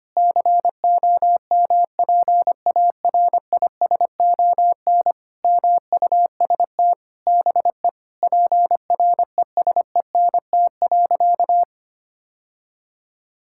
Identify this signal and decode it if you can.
Morse code, heard as a keyed tone, about 25 wpm, 700 Hz. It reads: COMPARISON MUST BE PRESENT.
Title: Sentences from Top 1600 Words 25wpm